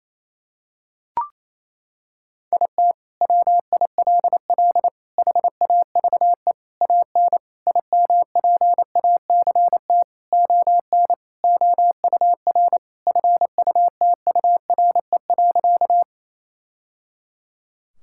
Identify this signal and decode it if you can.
Morse code, heard as a keyed tone, about 27 wpm, 700 Hz. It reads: IT WILL HAVE AN IMPACT ON OUR FUTURE.
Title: Sentences from Top 1000 Words - Review Part 6 - 28wpm